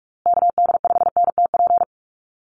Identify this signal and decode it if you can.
Morse code, heard as a keyed tone, about 44 wpm, 700 Hz. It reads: KD5NTP
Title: USA Call Signs Repeated 2x 45wpm